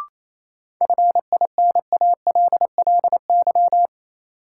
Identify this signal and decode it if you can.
Morse code, heard as a keyed tone, about 28 wpm, 700 Hz. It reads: FINALLY
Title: Sentences from Top 1000 Words - New Word Drill - 28wpm